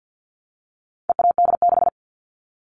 Morse code, heard as a keyed tone, about 50 wpm, 700 Hz. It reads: EAD6
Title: All Cut Numbers - 4-Digit 50wpm